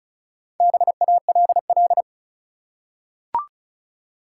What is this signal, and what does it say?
Morse code, heard as a keyed tone, about 36 wpm, 700 Hz. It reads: BALL
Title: Four Letter Words - 35wpm